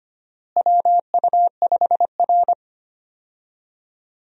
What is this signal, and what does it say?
Morse code, heard as a keyed tone, about 25 wpm, 700 Hz. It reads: WU5R
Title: USA Call Signs - 4 Characters - Repeated 3x - 25wpm